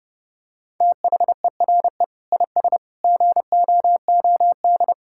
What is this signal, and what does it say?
Morse code, heard as a keyed tone, about 32 wpm, 700 Hz. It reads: THERE IS GOOD
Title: Sentences from Top 1000 Words - Review Part 5 - 30wpm